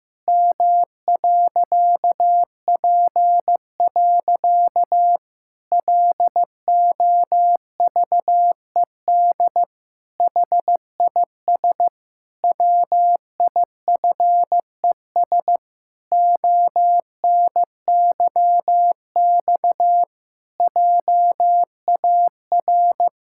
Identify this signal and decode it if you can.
Morse code, heard as a keyed tone, about 15 wpm, 700 Hz. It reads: M.P. LOVED HIS WIFES ONYX JAR
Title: Pangram Challenge v2 - 15wpm